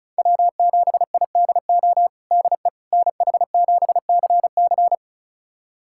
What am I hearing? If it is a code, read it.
Morse code, heard as a keyed tone, about 36 wpm, 700 Hz. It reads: W7IDO DE NH7CC
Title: USA Callsign DE Callsign - 5 Characters - 35wpm